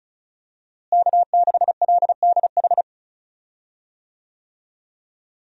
Morse code, heard as a keyed tone, about 36 wpm, 700 Hz. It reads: K6LDH
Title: USA Call Signs - 5 Characters - Repeated 2x - 35wpm